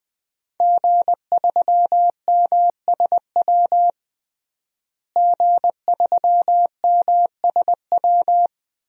Morse code, heard as a keyed tone, about 20 wpm, 700 Hz. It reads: G3MSW G3MSW
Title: DXCC-100 Call Signs - 5 Characters - Repeated 3x - 20wpm